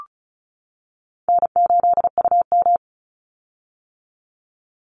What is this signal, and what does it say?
Morse code, heard as a keyed tone, about 34 wpm, 700 Hz. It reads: N8UM
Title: Call Signs - 35wpm